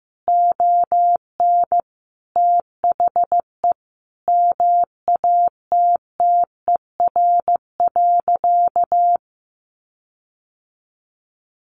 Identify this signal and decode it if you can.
Morse code, heard as a keyed tone, about 14 wpm, 700 Hz. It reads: ON THE MATTER.
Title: Sentences from Top 1300 Words 15wpm